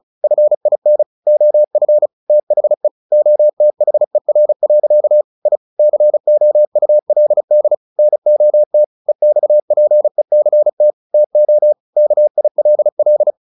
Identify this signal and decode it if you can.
Morse code, heard as a keyed tone, about 35 wpm, 600 Hz. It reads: FIN OF THE OTHER. I COULD NOT EXPECT TO KILL
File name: the old man and the sea 2 35wpm